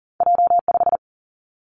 Morse code, heard as a keyed tone, about 40 wpm, 700 Hz. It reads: J 5
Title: Two Challenging Letter-Number Combinations 40wpm